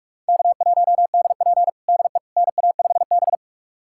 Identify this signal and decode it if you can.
Morse code, heard as a keyed tone, about 46 wpm, 700 Hz. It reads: K1DP DE NA5B
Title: USA Callsign DE Callsign - 4 Characters - 45wpm